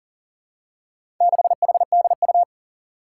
Morse code, heard as a keyed tone, about 38 wpm, 700 Hz. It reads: 6HDU
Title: Four Challenging Letter-Number Combinations 40wpm